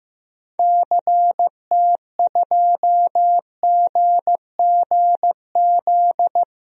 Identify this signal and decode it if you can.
Morse code, heard as a keyed tone, about 15 wpm, 700 Hz. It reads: CT2GGZ
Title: DXCC-100 Call Signs - 6 Characters - 15wpm